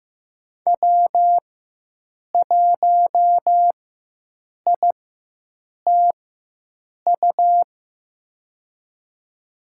Morse code, heard as a keyed tone, about 15 wpm, spaced as for 7 wpm, 700 Hz. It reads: W1ITU
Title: USA Call Signs - 5 Characters - 4x Character Spacing - 15wpm